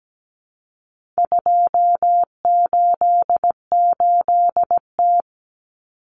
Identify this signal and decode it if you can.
Morse code, heard as a keyed tone, about 17 wpm, 700 Hz. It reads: 288T
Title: Common Cut Numbers - 4-Digit 17wpm